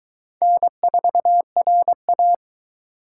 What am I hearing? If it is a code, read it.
Morse code, heard as a keyed tone, about 23 wpm, 700 Hz. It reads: N4RA